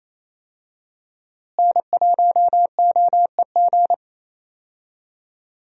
Morse code, heard as a keyed tone, about 28 wpm, 700 Hz. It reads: N1OEG
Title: USA Call Signs - 5 Characters - Repeated 2x - 28wpm